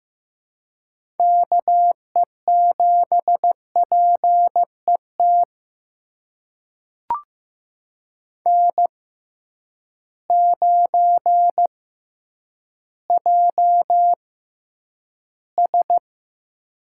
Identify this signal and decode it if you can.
Morse code, heard as a keyed tone, about 15 wpm, spaced as for 9 wpm, 700 Hz. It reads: KE7PET N9JS
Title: USA Call Signs - 6x Character Spacing - 15wpm